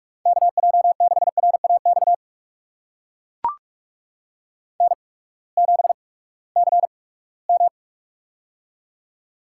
Morse code, heard as a keyed tone, about 42 wpm, spaced as for 22 wpm, 700 Hz. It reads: KJ6RAX N7CM
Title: USA Call Signs - 8x Character Spacing - 45wpm